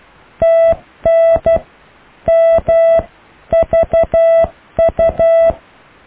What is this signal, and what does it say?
Morse code, heard as a keyed tone, about 12 wpm, 660 Hz. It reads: TN MVU
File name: cw / IJLUM+CDRO+EKTPV+BFHNW